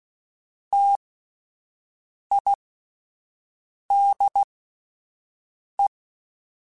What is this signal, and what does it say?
Morse code, heard as a keyed tone, about 16 wpm, spaced as for 5 wpm, 800 Hz. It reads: TIDE